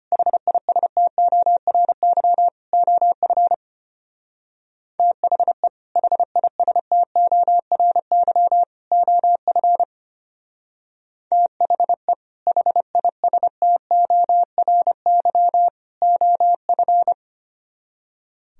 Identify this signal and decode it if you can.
Morse code, heard as a keyed tone, about 32 wpm, 700 Hz. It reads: HISTORY OF THE HISTORY OF THE HISTORY OF